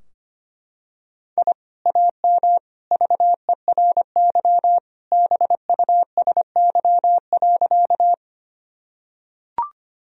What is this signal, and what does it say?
Morse code, heard as a keyed tone, about 25 wpm, 700 Hz. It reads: I AM VERY BUSY.
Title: Sentences from Top 1000 Words - Review Part 3 - 25wpm